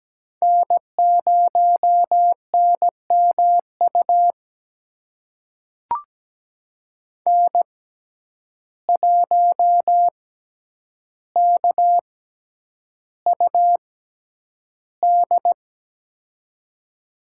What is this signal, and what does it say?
Morse code, heard as a keyed tone, about 17 wpm, spaced as for 9 wpm, 700 Hz. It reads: N0NMU N1KUD
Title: USA Call Signs - 5 Characters - 6x Character Spacing - 17wpm